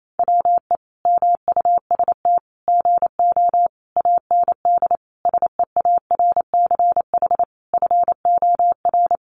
Morse code, heard as a keyed tone, about 28 wpm, 700 Hz. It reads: WE MUST GO AND SEARCH FOR
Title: Sentences from Top 800 Words 28wpm